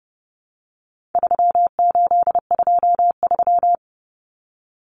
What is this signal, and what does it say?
Morse code, heard as a keyed tone, about 30 wpm, 700 Hz. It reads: 3823
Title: Common Cut Numbers - 4-Digit 30wpm